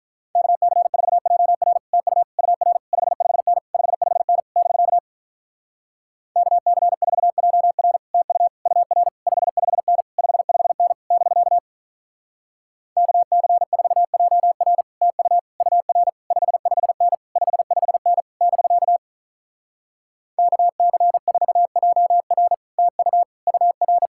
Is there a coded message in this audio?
Yes: Morse code from 0.4 to 24.1 s, heard as a keyed tone, about 47 wpm, 700 Hz. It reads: KC4JR TU UR 55N 55N <BK> KC4JR TU UR 55N 55N <BK> KC4JR TU UR 55N 55N <BK> KC4JR TU UR